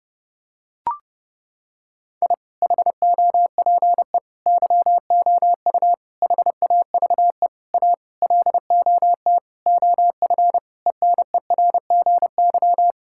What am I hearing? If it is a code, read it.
Morse code, heard as a keyed tone, about 32 wpm, 700 Hz. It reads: I HOPE YOU HAVE A LOT OF ENERGY
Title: Sentences from Top 1000 Words - Review Part 6 - 30wpm